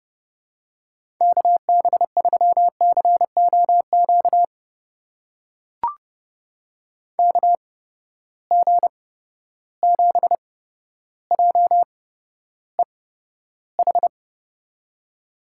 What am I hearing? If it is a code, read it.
Morse code, heard as a keyed tone, about 30 wpm, spaced as for 13 wpm, 700 Hz. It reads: KB3COQ KG7JEH
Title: USA Call Signs - 6 Characters - 8x Character Spacing - 30wpm